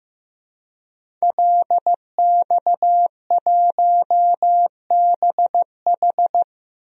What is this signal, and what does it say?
Morse code, heard as a keyed tone, about 15 wpm, 700 Hz. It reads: LX1BH
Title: DXCC-100 Call Signs - 5 Characters - 15wpm